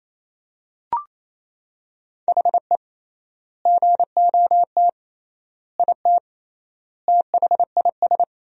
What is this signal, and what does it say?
Morse code, heard as a keyed tone, about 27 wpm, 700 Hz. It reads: HE GOT IT THIS
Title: 3x Word-Spacing - Sentences from Top 400 Words - 28wpm